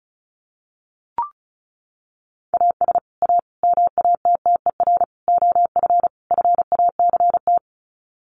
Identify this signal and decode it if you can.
Morse code, heard as a keyed tone, about 34 wpm, 700 Hz. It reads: AS A MATTER OF FACT
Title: Sets of 5 Words 35wpm